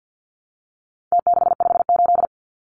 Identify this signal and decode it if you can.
Morse code, heard as a keyed tone, about 50 wpm, 700 Hz. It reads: T658
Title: Numbers - Contest Style - 4-Digit 50wpm